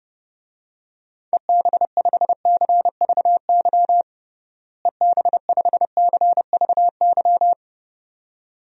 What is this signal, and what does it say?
Morse code, heard as a keyed tone, about 30 wpm, 700 Hz. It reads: EB5CVY EB5CVY
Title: DXCC-100 Call Signs - 6 Characters - Repeated 2x - 30wpm